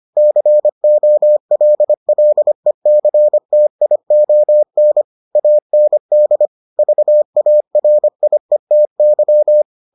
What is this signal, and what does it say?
Morse code, heard as a keyed tone, about 25 wpm, 600 Hz. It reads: COLLECTION AND VARIETY